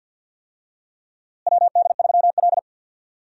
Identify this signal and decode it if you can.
Morse code, heard as a keyed tone, about 45 wpm, 700 Hz. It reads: WD3L
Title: USA Call Signs - 4 Characters - Repeated 2x - 50wpm